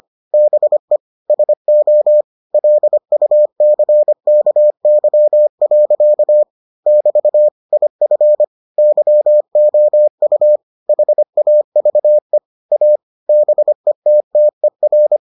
Morse code, heard as a keyed tone, about 25 wpm, 600 Hz. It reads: BE SO LUCKY. <BT> IF YOU HAVE A BETTER